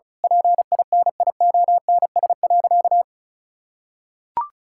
Morse code, heard as a keyed tone, about 34 wpm, 700 Hz. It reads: PINIONS.
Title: Pangram Challenge v2 - 35wpm